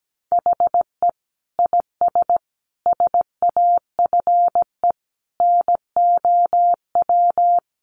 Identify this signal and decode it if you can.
Morse code, heard as a keyed tone, about 17 wpm, 700 Hz. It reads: HE IS SAFE NOW
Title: Sentences from Top 900 Words 17wpm